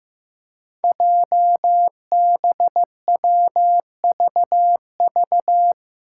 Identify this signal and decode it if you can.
Morse code, heard as a keyed tone, about 15 wpm, 700 Hz. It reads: JBWVV
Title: Five Challenging Letter-Number Combinations 15wpm